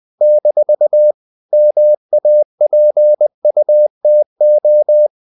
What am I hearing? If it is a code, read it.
Morse code, heard as a keyed tone, about 20 wpm, 600 Hz. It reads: - MAPUTO
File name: CW / Practice / NationalCapitals20